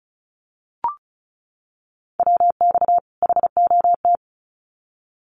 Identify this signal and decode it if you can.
Morse code, heard as a keyed tone, about 34 wpm, 700 Hz. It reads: WX HOT
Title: Common QSO Phrases 35wpm